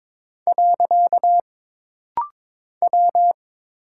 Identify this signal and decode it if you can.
Morse code, heard as a keyed tone, about 22 wpm, 700 Hz. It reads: . W
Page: This rapid-fire timing is geared to push your ICR (Instant Character Recognition) skills.